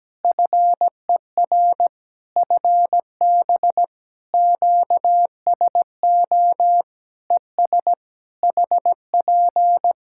Morse code, heard as a keyed tone, about 17 wpm, 700 Hz. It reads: FER FB QSO ES HP
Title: Practice QSO - Exchange 4 - Ending - 17wpm